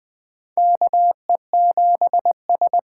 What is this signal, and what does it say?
Morse code, heard as a keyed tone, about 20 wpm, 700 Hz. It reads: KE7S
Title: USA Call Signs - 6 Characters - 8x Character Spacing - 20wpm